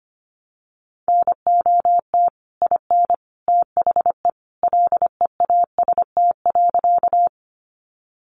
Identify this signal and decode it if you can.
Morse code, heard as a keyed tone, about 25 wpm, 700 Hz. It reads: NOT IN THE LEAST.
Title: Sentences from Top 700 Words 25wpm